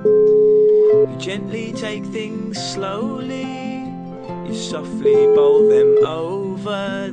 Ringback tone (Europe, 425 Hz), with noise.